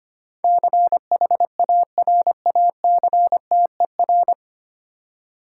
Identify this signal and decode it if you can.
Morse code, heard as a keyed tone, about 25 wpm, 700 Hz. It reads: CHARACTER
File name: practice / 25wpm / 9 Letter Words - First 3 Letters Given - 25wpm